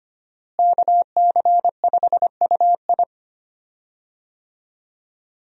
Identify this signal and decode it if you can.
Morse code, heard as a keyed tone, about 25 wpm, 700 Hz. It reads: KC5UI
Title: USA Call Signs - 5 Characters - Repeated 3x - 25wpm